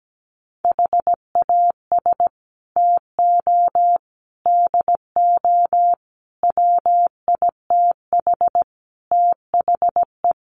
Morse code, heard as a keyed tone, about 17 wpm, 700 Hz. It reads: HAS TO DO WITH THE
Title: Sets of 5 Words 17wpm